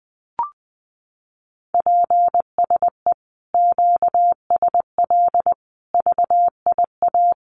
Morse code, heard as a keyed tone, about 20 wpm, 700 Hz. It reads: PSE QSL VIA
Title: Common QSO Phrases Encore 20wpm